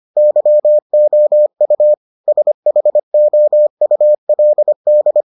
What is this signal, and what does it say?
Morse code, heard as a keyed tone, about 25 wpm, 600 Hz. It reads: YOU SHOULD